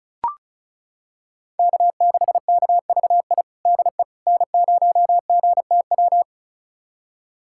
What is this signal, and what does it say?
Morse code, heard as a keyed tone, about 36 wpm, 700 Hz. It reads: K6KVI DE N0GTW
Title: USA Callsign DE Callsign - 5 Characters - 35wpm